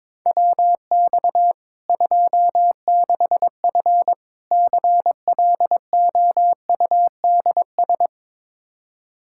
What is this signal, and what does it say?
Morse code, heard as a keyed tone, about 22 wpm, 700 Hz. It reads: WX 26F CLOUDS
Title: Practice QSO - Exchange 2 - Info - 22wpm